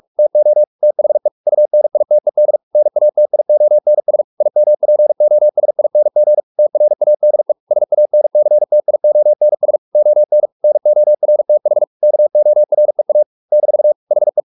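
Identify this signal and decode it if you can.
Morse code, heard as a keyed tone, about 44 wpm, 600 Hz. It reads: TO THE UNITED NATIONS IMPOSING TRADE SANCTIONS ON NORTH KOREA - HE
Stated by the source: Morse practice at 45 wpm, the novel Never by Ken Follett in English, from CW med Gnister.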